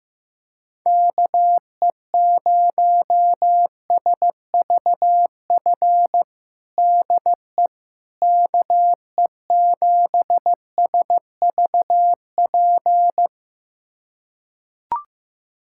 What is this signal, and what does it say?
Morse code, heard as a keyed tone, about 15 wpm, 700 Hz. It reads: KE0SVF DE KE7SVP
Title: USA Callsign DE Callsign - 6 Characters - 15wpm